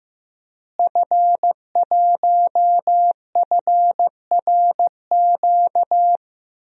Morse code, heard as a keyed tone, about 15 wpm, 700 Hz. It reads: F1FRQ
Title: DXCC-100 Call Signs - 5 Characters - Repeated 3x - 15wpm